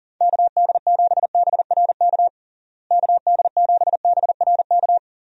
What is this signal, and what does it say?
Morse code, heard as a keyed tone, about 40 wpm, 700 Hz. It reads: KD7BRK KD7BRK
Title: USA Call Signs - 6 Characters - Repeated 3x - 40wpm